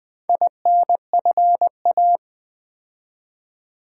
Morse code, heard as a keyed tone, about 20 wpm, 700 Hz. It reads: INFA
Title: Build A Word - 2 Letters At A Time - 6-Letter Words - 20wpm